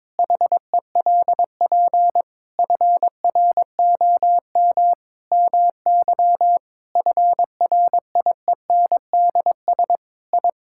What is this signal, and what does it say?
Morse code, heard as a keyed tone, about 22 wpm, 700 Hz. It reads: HELP FROM MY FRIENDS I
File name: Sentences with 7-Word Phrases - 22wpm